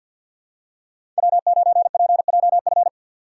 Morse code, heard as a keyed tone, about 53 wpm, 700 Hz. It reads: W9PJF
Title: USA Call Signs - 5 Characters - 3x Character Spacing - 50wpm